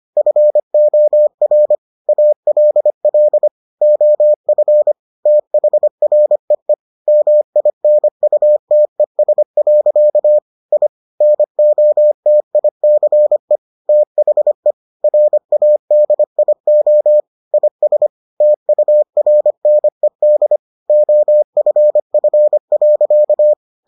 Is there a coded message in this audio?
Yes: Morse code from 0.2 to 23.5 s, heard as a keyed tone, about 25 wpm, 600 Hz. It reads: FOR ALL OF THREE MINUTES. I NOTICE THE RADIO IS TURNED OFF.